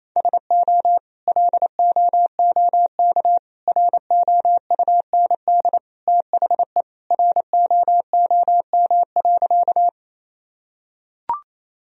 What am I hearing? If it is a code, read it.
Morse code, heard as a keyed tone, about 28 wpm, 700 Hz. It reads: SO LOOK ROUND THE ROOM.
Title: Sentences from Top 1000 Words - Review Part 1 - 28wpm